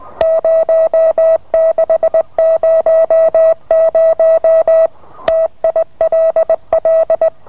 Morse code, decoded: 0600 TILL